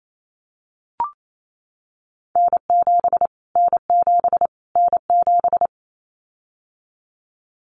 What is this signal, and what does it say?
Morse code, heard as a keyed tone, about 28 wpm, 700 Hz. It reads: N7 N7 N7
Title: Callsign Prefixes for Top 25 DXCC - Repeated 3x - 28wpm